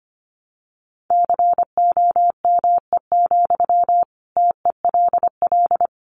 Morse code, heard as a keyed tone, about 25 wpm, 700 Hz. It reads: COME, TELL
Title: Sentences from Top 1200 Words 25wpm